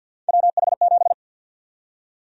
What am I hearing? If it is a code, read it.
Morse code, heard as a keyed tone, about 48 wpm, 700 Hz. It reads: WH7
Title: USA Call Sign Prefixes - 50wpm